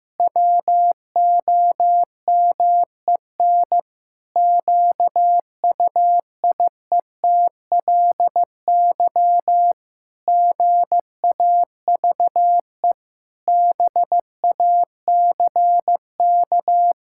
Morse code, heard as a keyed tone, about 15 wpm, 700 Hz. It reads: WOMEN QUIETLY GAVE BACK
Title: Pangram Challenge v2 - 15wpm